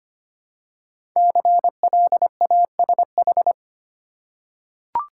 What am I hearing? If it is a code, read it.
Morse code, heard as a keyed tone, about 25 wpm, 700 Hz. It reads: CLASH
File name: 5 Letter Words - First 3 Letters Given - 25wpm